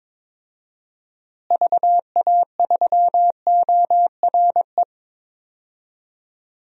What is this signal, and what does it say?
Morse code, heard as a keyed tone, about 21 wpm, 700 Hz. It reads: VA3ORE
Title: DXCC-100 Call Signs - 6 Characters - Repeated 2x - 22wpm